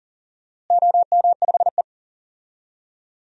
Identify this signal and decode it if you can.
Morse code, heard as a keyed tone, about 40 wpm, 700 Hz. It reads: OM5E
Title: DXCC-100 Call Signs - 4 Characters - Repeated 2x - 40wpm